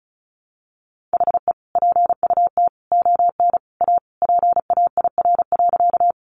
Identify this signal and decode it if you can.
Morse code, heard as a keyed tone, about 34 wpm, 700 Hz. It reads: HE PUT ON A PAIR.
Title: Sentences from Top 1200 Words 35wpm